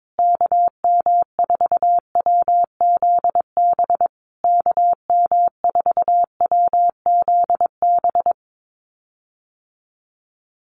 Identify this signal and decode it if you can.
Morse code, heard as a keyed tone, about 22 wpm, 700 Hz. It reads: KM4WZB KM4WZB
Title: USA Call Signs Repeated 3x 22wpm